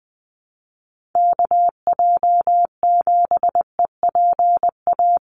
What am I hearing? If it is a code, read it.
Morse code, heard as a keyed tone, about 20 wpm, 700 Hz. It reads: KJ7EPA
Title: USA Call Signs Repeated 2x 20wpm